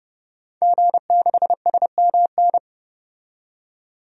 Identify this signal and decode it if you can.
Morse code, heard as a keyed tone, about 32 wpm, 700 Hz. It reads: G6SMN
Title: DXCC-100 Call Signs - 5 Characters - 30wpm